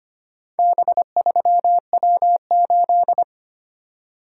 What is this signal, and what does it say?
Morse code, heard as a keyed tone, about 25 wpm, 700 Hz. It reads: B3W8
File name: practice / 25wpm / Four Letter-Number Combinations - 25wpm